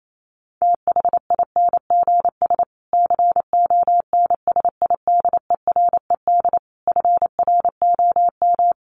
Morse code, heard as a keyed tone, about 28 wpm, 700 Hz. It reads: THINGS CONSIDERED FROM
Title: Sets of 5 Words Encore 28wpm